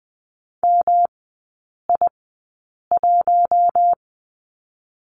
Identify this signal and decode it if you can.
Morse code, heard as a keyed tone, about 20 wpm, spaced as for 8 wpm, 700 Hz. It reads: MI1
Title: 114 Lesson 19e - Letter w - Three Characters 20wpm